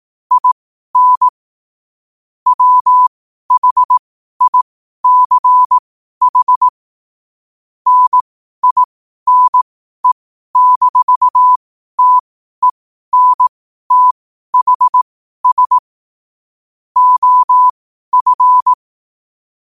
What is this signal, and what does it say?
Morse code, heard as a keyed tone, about 18 wpm, spaced as for 12 wpm, 1000 Hz. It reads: IN WHICH NINE-TENTHS OF
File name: qotdfiles / 12wpm